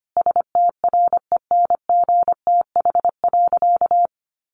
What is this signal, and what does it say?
Morse code, heard as a keyed tone, about 25 wpm, 700 Hz. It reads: STRENGTH.
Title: Sentences from Top 1300 Words 25wpm